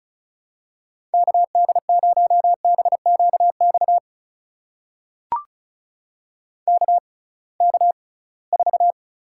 Morse code, heard as a keyed tone, about 36 wpm, spaced as for 23 wpm, 690 Hz. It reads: KD0BQX KK4
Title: USA Call Signs - 6 Characters - 6x Character Spacing - 35wpm